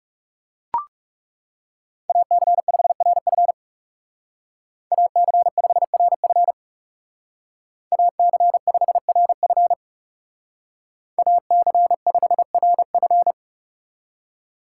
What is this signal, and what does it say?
Morse code, heard as a keyed tone, about 40 wpm, 700 Hz. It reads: AC5RF AC5RF AC5RF AC5RF